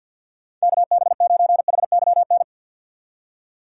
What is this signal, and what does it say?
Morse code, heard as a keyed tone, about 48 wpm, 700 Hz. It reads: KB9HYN
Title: USA Call Signs - 6 Characters - 4x Character Spacing - 50wpm